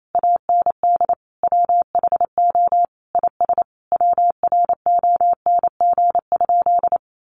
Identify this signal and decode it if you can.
Morse code, heard as a keyed tone, about 28 wpm, 700 Hz. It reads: AND WHO IS WRONG?
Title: Sentences from Top 900 Words 28wpm